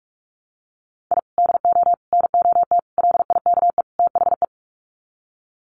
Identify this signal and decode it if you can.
Morse code, heard as a keyed tone, about 44 wpm, 700 Hz. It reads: I DO NOT LIKE THE
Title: Sets of 5 Words Encore 45wpm